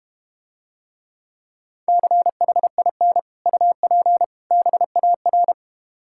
Morse code, heard as a keyed tone, about 32 wpm, 700 Hz. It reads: CHIN UP BAR